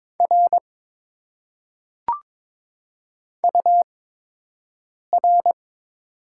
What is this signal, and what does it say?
Morse code, heard as a keyed tone, about 21 wpm, spaced as for 6 wpm, 700 Hz. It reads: R UR
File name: Top 90 QSO Elements - 8x Character Spacing - 22wpm